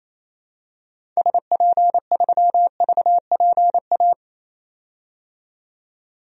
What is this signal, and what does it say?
Morse code, heard as a keyed tone, about 28 wpm, 700 Hz. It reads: SP3VPA
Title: DXCC-100 Call Signs - 6 Characters - 28wpm